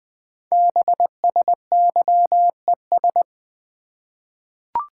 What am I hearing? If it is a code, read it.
Morse code, heard as a keyed tone, about 20 wpm, 700 Hz. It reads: BSYES
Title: Code Groups - 5 Letters - 20wpm